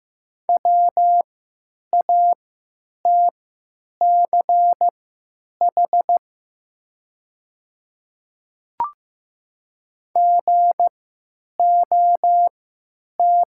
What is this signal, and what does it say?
Morse code, heard as a keyed tone, about 15 wpm, spaced as for 9 wpm, 700 Hz. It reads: WATCH GOT
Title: 3x Character Spacing - Top 300 Words - 15wpm